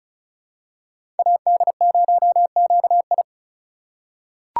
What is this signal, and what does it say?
Morse code, heard as a keyed tone, about 36 wpm, 700 Hz. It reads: AD0QI
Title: USA Call Signs - 5 Characters - 3x Character Spacing - 35wpm